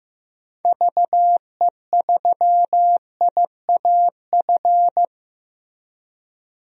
Morse code, heard as a keyed tone, about 15 wpm, 700 Hz. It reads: VE3IAF
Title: DXCC-100 Call Signs - 6 Characters - Repeated 2x - 15wpm